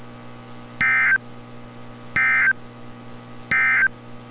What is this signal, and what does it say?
Emergency Alert System (SAME) tones, heard as harsh buzzing bursts. It is heard from 0.7 to 4.0 s.
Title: NOAA Weather Radio - Altoona, PA